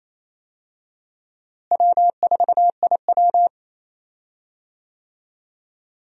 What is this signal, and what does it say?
Morse code, heard as a keyed tone, about 28 wpm, 700 Hz. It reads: W4IW
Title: USA Call Signs - 4 Characters - Repeated 2x - 28wpm